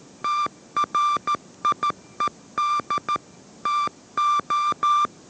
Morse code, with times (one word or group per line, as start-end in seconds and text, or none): 0.2-3.2 TRIED
3.7-5.1 TO